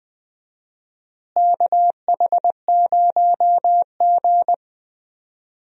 Morse code, heard as a keyed tone, about 20 wpm, 700 Hz. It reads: KH0G
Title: USA Call Signs - 4 Characters - Repeated 3x - 20wpm